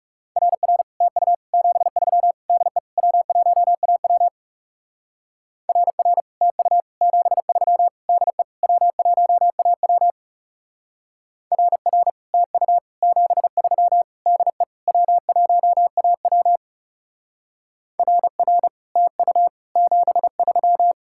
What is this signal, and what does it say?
Morse code, heard as a keyed tone, about 37 wpm, 700 Hz. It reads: RR TU 73 DE W1AW RR TU 73 DE W1AW RR TU 73 DE W1AW RR TU 73
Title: Speed-Racing Common QSO Phrases - 30wpm